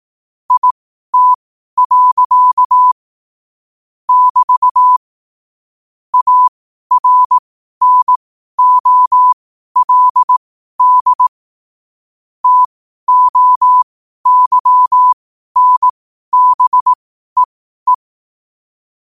Morse code, heard as a keyed tone, about 18 wpm, spaced as for 12 wpm, 1000 Hz. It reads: IT. <BT> ARNOLD TOYNBEE
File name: qotdfiles / 12wpm